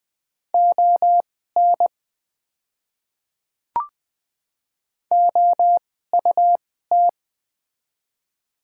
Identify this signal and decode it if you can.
Morse code, heard as a keyed tone, about 20 wpm, spaced as for 14 wpm, 700 Hz. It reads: ON OUT
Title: 2x Character Spacing - Top 100 Words - 20wpm